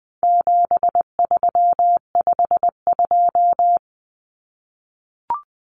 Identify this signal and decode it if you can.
Morse code, heard as a keyed tone, about 20 wpm, 700 Hz. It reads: 7352
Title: Numbers - 4-Digit 20wpm